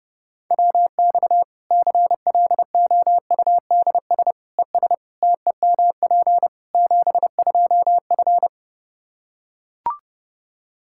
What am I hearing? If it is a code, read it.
Morse code, heard as a keyed tone, about 30 wpm, 700 Hz. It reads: WX CLOUDS ES TEMP 72F
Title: Practice QSO - Exchange 2 - Info - 30wpm